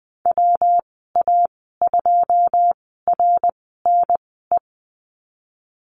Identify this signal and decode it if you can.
Morse code, heard as a keyed tone, about 20 wpm, spaced as for 15 wpm, 700 Hz. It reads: WA2RNE
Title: 140 Lesson 23g - Letter f - Call Signs 20wpm